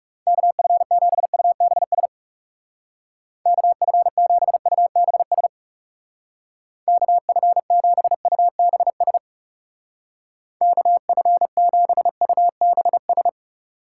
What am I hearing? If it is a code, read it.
Morse code, heard as a keyed tone, about 38 wpm, 700 Hz. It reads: KF7UBS KF7UBS KF7UBS KF7UBS